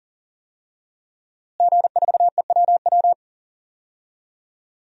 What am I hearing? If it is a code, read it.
Morse code, heard as a keyed tone, about 40 wpm, 700 Hz. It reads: G4EWW
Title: DXCC-100 Call Signs - 5 Characters - Repeated 2x - 40wpm